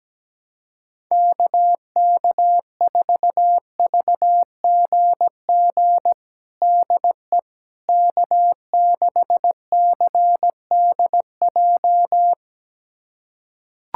Morse code, heard as a keyed tone, about 17 wpm, 700 Hz. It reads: KK4VGG DE K6CDJ
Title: USA Callsign DE Callsign 17wpm